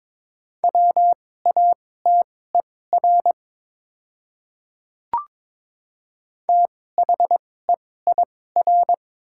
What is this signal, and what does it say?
Morse code, heard as a keyed tone, about 22 wpm, spaced as for 16 wpm, 700 Hz. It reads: WATER THEIR
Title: 2x Character Spacing - Top 100 Words - 22wpm